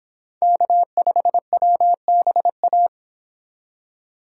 Morse code, heard as a keyed tone, about 29 wpm, 700 Hz. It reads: K5WBA